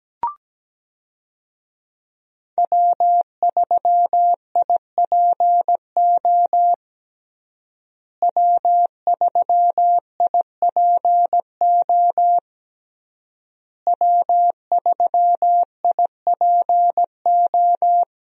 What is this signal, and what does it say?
Morse code, heard as a keyed tone, about 17 wpm, 700 Hz. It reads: W3IPO W3IPO W3IPO
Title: USA Call Signs - 5 Characters - Repeated 3x - 17wpm